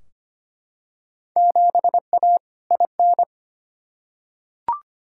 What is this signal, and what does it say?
Morse code, heard as a keyed tone, about 25 wpm, 700 Hz. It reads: 7A IN
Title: Field Day Exchanges Encore - 25wpm